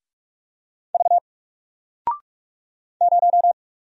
Morse code, heard as a keyed tone, about 44 wpm, 700 Hz. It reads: V 0
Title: Single Letter-Number 45wpm